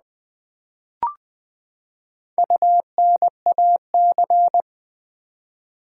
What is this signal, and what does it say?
Morse code, heard as a keyed tone, about 20 wpm, 700 Hz. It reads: UNAC
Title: Build A Word - 4 Letters At A Time - 12-Letter Words - 20wpm